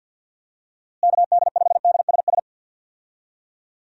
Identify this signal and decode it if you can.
Morse code, heard as a keyed tone, about 50 wpm, 700 Hz. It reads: KD5DSS
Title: USA Call Signs - 6 Characters - Repeated 2x - 50wpm